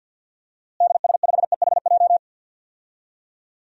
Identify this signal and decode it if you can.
Morse code, heard as a keyed tone, about 52 wpm, 700 Hz. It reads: DS5EHJ
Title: DXCC-100 Call Signs - 6 Characters - Repeated 2x - 50wpm